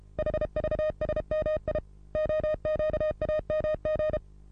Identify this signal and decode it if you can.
Morse code, heard as a keyed tone, about 27 wpm, 620 Hz. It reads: HVSMI OQAMG
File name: repos / Morsekurs / DJ4UF Morsen27